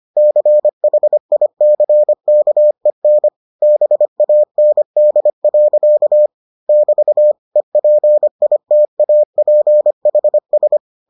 Morse code, heard as a keyed tone, about 25 wpm, 600 Hz. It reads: CHICKEN BAND. <BT> EPITAPHS